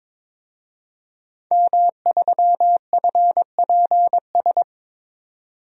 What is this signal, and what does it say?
Morse code, heard as a keyed tone, about 22 wpm, 700 Hz. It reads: M3FPS
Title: DXCC-100 Call Signs - 5 Characters - 22wpm